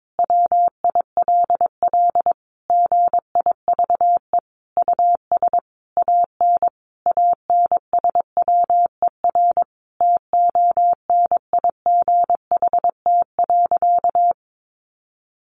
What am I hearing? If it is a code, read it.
Morse code, heard as a keyed tone, about 22 wpm, 700 Hz. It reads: WILL GIVE US AN ANSWER TONIGHT.
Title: Sentences from Top 600 Words 22wpm